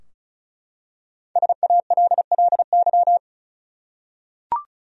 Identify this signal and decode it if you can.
Morse code, heard as a keyed tone, about 36 wpm, 700 Hz. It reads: SALLY
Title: USA Common Names - Max 5 Letters - 35wpm